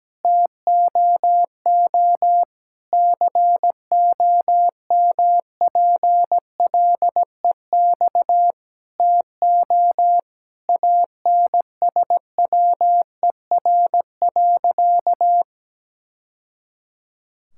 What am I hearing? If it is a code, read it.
Morse code, heard as a keyed tone, about 17 wpm, 700 Hz. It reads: TOO COMPLEX TO ANSWER.
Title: Sentences from Top 1000 Words - Review Part 6 - 17wpm